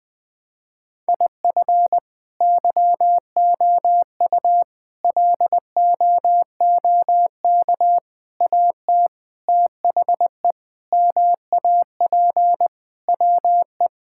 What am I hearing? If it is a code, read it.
Morse code, heard as a keyed tone, about 21 wpm, 700 Hz. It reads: IF YOU LOOK AT THE MAP WE
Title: Sentences with 5-Word Phrases - 20wpm